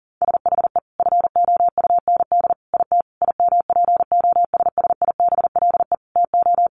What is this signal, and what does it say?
Morse code, heard as a keyed tone, about 40 wpm, 700 Hz. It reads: SHE FOUND IT IMPOSSIBLE TO
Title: Sentences from Top 1100 Words 40wpm